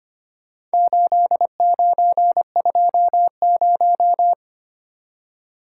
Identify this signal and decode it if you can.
Morse code, heard as a keyed tone, about 25 wpm, 700 Hz. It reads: 8920
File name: practice / 25wpm / Four Letter-Number Combinations - 25wpm